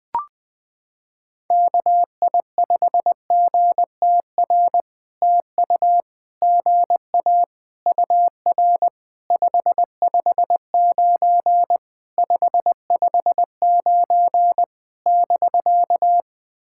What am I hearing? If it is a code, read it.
Morse code, heard as a keyed tone, about 20 wpm, 700 Hz. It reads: KI5GTR TU GA UR 559 559 <BK>
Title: POTA QSO Practice - 20wpm